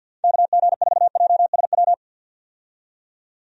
Morse code, heard as a keyed tone, about 51 wpm, 700 Hz. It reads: KG4JSW
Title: USA Call Signs - 6 Characters - 50wpm